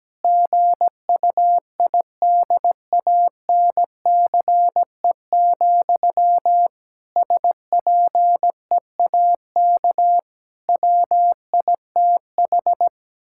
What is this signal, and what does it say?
Morse code, heard as a keyed tone, about 17 wpm, 700 Hz. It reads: GUIDANCE, SPEAK WITH
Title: Sentences from Top 1600 Words 17wpm